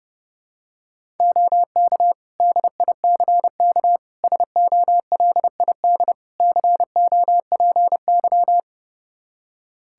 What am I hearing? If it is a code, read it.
Morse code, heard as a keyed tone, about 32 wpm, 700 Hz. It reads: OK DICK SOLID COPY
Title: Practice QSO - Exchange 3 - More Info - 30wpm